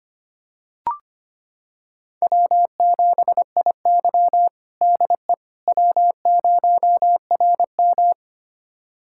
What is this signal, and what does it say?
Morse code, heard as a keyed tone, about 25 wpm, 700 Hz. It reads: W7IY DE W0RM
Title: USA Callsign DE Callsign - 4 Characters - 25wpm